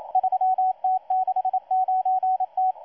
Morse code, heard as a keyed tone, about 27 wpm, 750 Hz, with radio noise. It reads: 3T69T
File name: M12-2016-07-07-1310utc-13926khz-error-restart